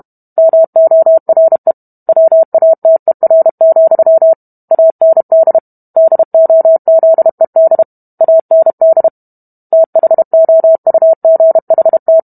Morse code, heard as a keyed tone, about 32 wpm, 650 Hz. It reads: MORE WATER, AND DOZED AND THOUGHT